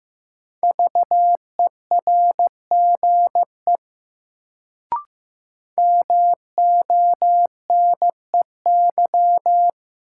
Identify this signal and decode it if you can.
Morse code, heard as a keyed tone, about 15 wpm, 700 Hz. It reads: VERGE MONEY
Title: Sets of 4 Words - New Word Drill - 15wpm